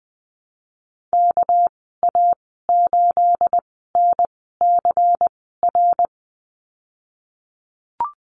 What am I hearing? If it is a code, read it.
Morse code, heard as a keyed tone, about 20 wpm, spaced as for 15 wpm, 700 Hz. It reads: KA8NCR
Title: 219 Lesson 35g - Number 0 - Call Signs 20wpm